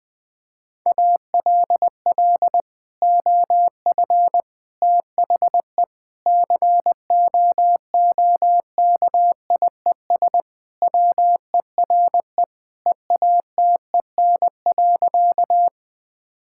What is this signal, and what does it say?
Morse code, heard as a keyed tone, about 20 wpm, 700 Hz. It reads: ALL OF THE COOKIES WERE EATEN.
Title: Sentences with 2-Word Phrases - Spoken Beginning - 20wpm